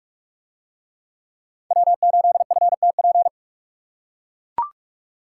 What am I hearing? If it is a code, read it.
Morse code, heard as a keyed tone, about 44 wpm, 700 Hz. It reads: W8FTP